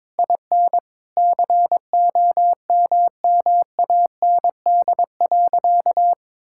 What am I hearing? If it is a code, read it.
Morse code, heard as a keyed tone, about 22 wpm, 700 Hz. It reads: IN COMMAND.
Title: Sentences from Top 1600 Words 22wpm